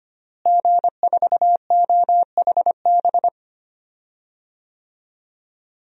Morse code, heard as a keyed tone, about 25 wpm, 700 Hz. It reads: G4OHB
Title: DXCC-100 Call Signs - 5 Characters - Repeated 2x - 25wpm